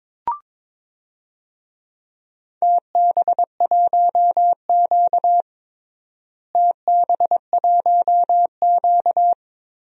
Morse code, heard as a keyed tone, about 23 wpm, 700 Hz. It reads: TB1Q TB1Q
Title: DXCC-100 Call Signs - 4 Characters - Repeated 2x - 22wpm